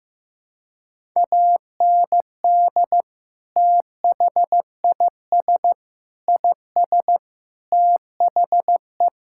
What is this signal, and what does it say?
Morse code, heard as a keyed tone, about 15 wpm, 700 Hz. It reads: AND THIS IS THE
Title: Sentences from Top 1000 Words - Review Part 5 - 15wpm